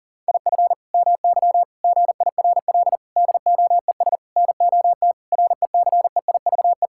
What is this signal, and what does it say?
Morse code, heard as a keyed tone, about 39 wpm, 700 Hz. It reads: IF MY GIRL DOES NOT RECEIVE
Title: Sentences from Top 1600 Words 40wpm